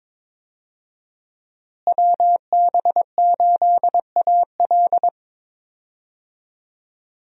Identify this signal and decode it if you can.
Morse code, heard as a keyed tone, about 22 wpm, 700 Hz. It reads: WB8AL
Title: USA Call Signs - 5 Characters - Repeated 2x - 22wpm